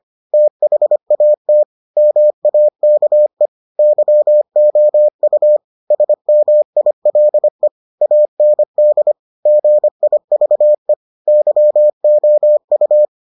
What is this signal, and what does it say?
Morse code, heard as a keyed tone, about 25 wpm, 600 Hz. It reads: THAT MAKE YOU SMILE AND GIVE YOU